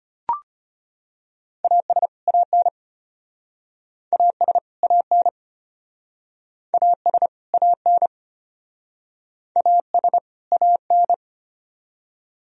Morse code, heard as a keyed tone, about 34 wpm, 700 Hz. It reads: AS AN AS AN AS AN AS AN